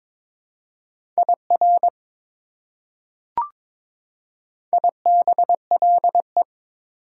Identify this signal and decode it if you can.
Morse code, heard as a keyed tone, about 22 wpm, 700 Hz. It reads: IR IBLE
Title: Prefixes and Suffixes 22wpm